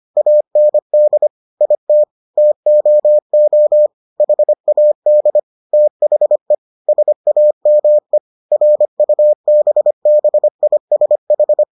Morse code, heard as a keyed tone, about 25 wpm, 600 Hz. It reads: AND IT TOO HAD THE SAME RUBBISH